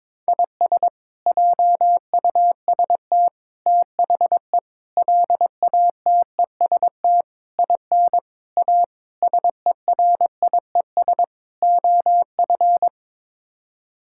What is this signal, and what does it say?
Morse code, heard as a keyed tone, about 22 wpm, 700 Hz. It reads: IS JUST THE LATEST IN A SERIES OF
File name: practice / 22wpm / Sets of 8 Words - 22wpm